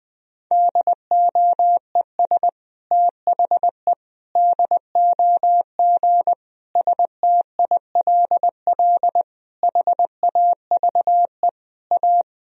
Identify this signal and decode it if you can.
Morse code, heard as a keyed tone, about 20 wpm, 700 Hz. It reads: DOES THE DOG STILL HAVE A
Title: Sentences from Top 1000 Words - Review Part 3 - 20wpm